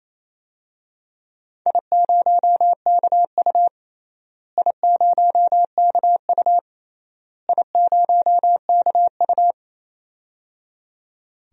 Morse code, heard as a keyed tone, about 28 wpm, 700 Hz. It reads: I0KU I0KU I0KU
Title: DXCC-100 Call Signs - 4 Characters - Repeated 3x - 28wpm